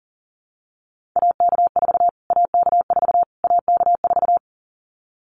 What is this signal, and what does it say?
Morse code, heard as a keyed tone, about 40 wpm, 700 Hz. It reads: AK4 AK4 AK4
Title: Callsign Prefixes for Top 25 DXCC - Repeated 3x - 40wpm